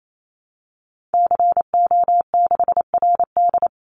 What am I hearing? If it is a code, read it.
Morse code, heard as a keyed tone, about 28 wpm, 700 Hz. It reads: CO6RD
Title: Call Signs - 28wpm